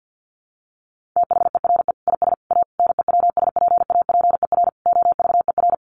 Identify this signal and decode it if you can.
Morse code, heard as a keyed tone, about 48 wpm, 700 Hz. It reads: THERE IS A NEWSPAPER OVER
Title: Sentences from Top 1100 Words 50wpm